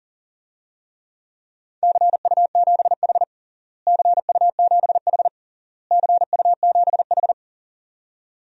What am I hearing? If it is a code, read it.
Morse code, heard as a keyed tone, about 40 wpm, 700 Hz. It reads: CU7H CU7H CU7H
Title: DXCC-100 Call Signs - 4 Characters - Repeated 3x - 40wpm